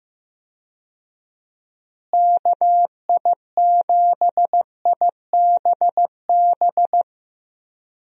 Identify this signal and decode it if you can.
Morse code, heard as a keyed tone, about 15 wpm, 700 Hz. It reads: KI7IBB
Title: USA Call Signs - 6 Characters - Repeated 2x - 15wpm